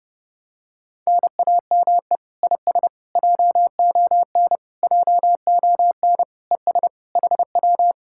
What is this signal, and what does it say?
Morse code, heard as a keyed tone, about 30 wpm, 700 Hz. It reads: NAME IS JON JON ES HW